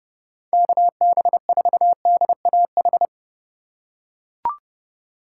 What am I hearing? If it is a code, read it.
Morse code, heard as a keyed tone, about 30 wpm, 700 Hz. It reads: KB4DAH